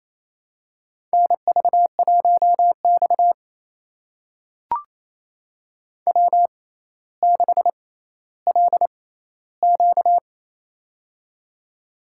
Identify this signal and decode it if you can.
Morse code, heard as a keyed tone, about 27 wpm, spaced as for 14 wpm, 700 Hz. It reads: NV1X W6LQ
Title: USA Call Signs - 4 Characters - 6x Character Spacing - 28wpm